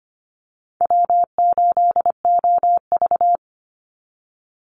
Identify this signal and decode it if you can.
Morse code, heard as a keyed tone, about 25 wpm, 700 Hz. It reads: W8OV
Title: Easy Call Signs Repeated 2x - 25wpm